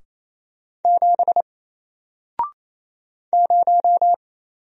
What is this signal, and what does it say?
Morse code, heard as a keyed tone, about 28 wpm, 700 Hz. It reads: 7 0
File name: Single-Letter-Number-28wpm